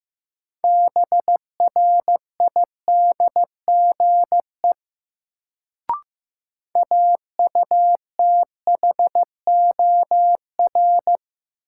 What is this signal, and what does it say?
Morse code, heard as a keyed tone, about 16 wpm, 700 Hz. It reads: BRIDGE AUTHOR
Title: Sentences from Top 1200 Words - New Word Drill - 15wpm